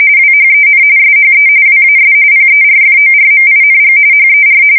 RTTY, about 45.45 baud. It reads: WHEN SHE WAS PERFORMING HER
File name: fsk170 / battleshipnj